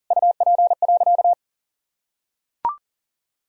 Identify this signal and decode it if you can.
Morse code, heard as a keyed tone, about 40 wpm, 700 Hz. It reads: UP.
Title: Sentences from Top 1000 Words - Review Part 1 - 40wpm